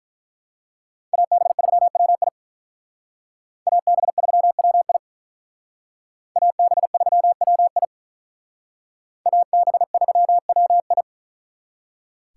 Morse code, heard as a keyed tone, about 44 wpm, 700 Hz. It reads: AB3WI AB3WI AB3WI AB3WI
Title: Speed Racing - USA 2x2 Call Signs 35wpm